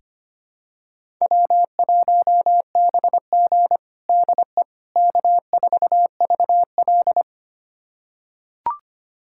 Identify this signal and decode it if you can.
Morse code, heard as a keyed tone, about 25 wpm, 700 Hz. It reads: W1BG DE K4VL
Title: USA Callsign DE Callsign - 4 Characters - 25wpm